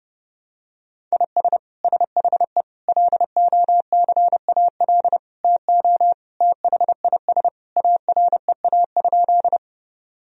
Morse code, heard as a keyed tone, about 32 wpm, 700 Hz. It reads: IS SHE LOCAL TO THIS AREA?
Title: Sentences from Top 1000 Words - Review Part 3 - 30wpm